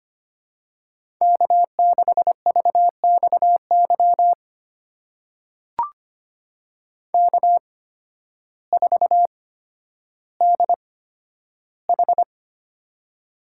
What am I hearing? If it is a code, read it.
Morse code, heard as a keyed tone, about 25 wpm, spaced as for 12 wpm, 700 Hz. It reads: K6VXY K4DH
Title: USA Call Signs - 8x Character Spacing - 25wpm